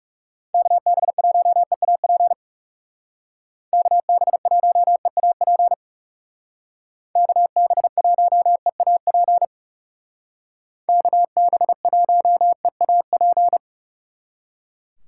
Morse code, heard as a keyed tone, about 36 wpm, 700 Hz. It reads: KB1EAP KB1EAP KB1EAP KB1EAP
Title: Speed Racing - USA 2x3 Call Signs 30wpm